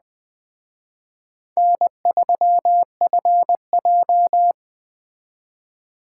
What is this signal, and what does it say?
Morse code, heard as a keyed tone, about 20 wpm, 700 Hz. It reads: N3FJ